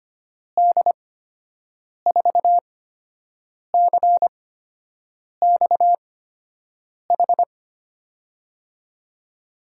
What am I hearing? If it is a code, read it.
Morse code, heard as a keyed tone, about 25 wpm, spaced as for 7 wpm, 700 Hz. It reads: D4CXH
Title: USA Call Signs - 8x Character Spacing - 25wpm